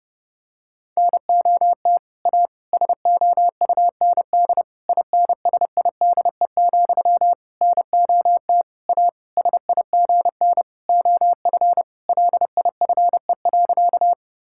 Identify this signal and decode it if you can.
Morse code, heard as a keyed tone, about 30 wpm, 700 Hz. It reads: NOT A SOUND INSIDE, NOT A SIGN OF LIFE.
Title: Sentences from Top 1000 Words - Review Part 6 - 30wpm